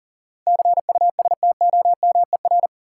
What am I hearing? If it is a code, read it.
Morse code, heard as a keyed tone, about 38 wpm, 700 Hz. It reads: CUSTOMER
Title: Sentences from Top 1100 Words - New Word Drill - 40wpm